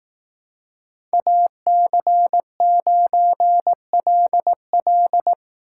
Morse code, heard as a keyed tone, about 18 wpm, 700 Hz. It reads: AC9LL
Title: Speed Racing - USA 2x2 Call Signs 15wpm